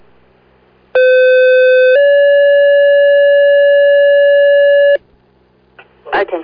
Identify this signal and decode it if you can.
Two-tone page, heard as a one-second tone, then a three-second tone, about 510 then 570 Hz, with speech. It is heard from 0.9 to 5.0 s.